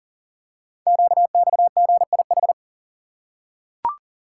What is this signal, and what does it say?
Morse code, heard as a keyed tone, about 38 wpm, 700 Hz. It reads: QXGIH
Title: Code Groups - 5 Letters - 40wpm